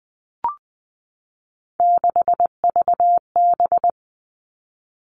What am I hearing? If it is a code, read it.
Morse code, heard as a keyed tone, about 20 wpm, 700 Hz. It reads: 6VB
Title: All Cut Numbers - 3-Digit 20wpm